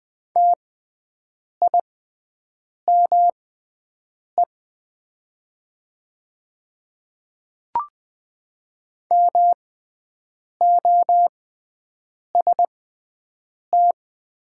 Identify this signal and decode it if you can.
Morse code, heard as a keyed tone, about 20 wpm, spaced as for 7 wpm, 700 Hz. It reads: TIME MOST
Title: 6x Character Spacing - Top 100 Words - 20wpm